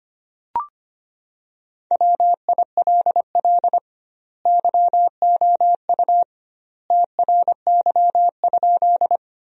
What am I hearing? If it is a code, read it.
Morse code, heard as a keyed tone, about 25 wpm, 700 Hz. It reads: WILL YOU TRY?
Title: 2x Word-Spacing - Sentences from Top 200 Words - 25wpm